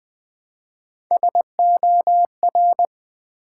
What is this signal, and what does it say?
Morse code, heard as a keyed tone, about 20 wpm, 700 Hz. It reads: SOR
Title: Build A Word - 1 Letter At A Time - 4-Letter Words - 20wpm